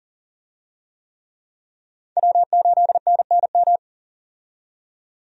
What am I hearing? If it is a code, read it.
Morse code, heard as a keyed tone, about 40 wpm, 700 Hz. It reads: W8NNM
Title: USA Call Signs - 5 Characters - 40wpm